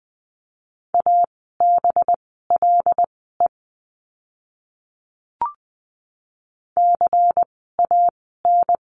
Morse code, heard as a keyed tone, about 20 wpm, spaced as for 15 wpm, 700 Hz. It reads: ABLE CAN
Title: 185 Lesson 30f - Letter b - Words 20wpm